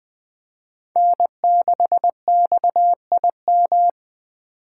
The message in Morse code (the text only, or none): N6XIM